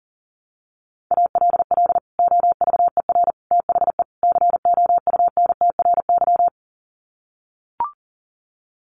Morse code, heard as a keyed tone, about 40 wpm, 700 Hz. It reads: ALL OVER THE COUNTRY
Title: Sets of 4 Words 40wpm